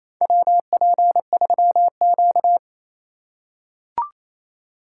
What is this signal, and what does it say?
Morse code, heard as a keyed tone, about 28 wpm, 700 Hz. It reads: WP3Q
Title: USA Call Signs - 3x Character Spacing - 28wpm